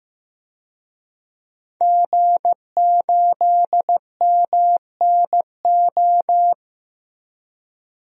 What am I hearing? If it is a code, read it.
Morse code, heard as a keyed tone, about 15 wpm, 700 Hz. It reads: G8MNO
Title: DXCC-100 Call Signs - 5 Characters - Repeated 2x - 15wpm